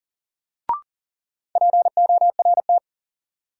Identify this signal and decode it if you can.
Morse code, heard as a keyed tone, about 40 wpm, 700 Hz. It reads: PORT
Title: Sentences from Top 1000 Words - Review Part 1 - New Word Drill - 40wpm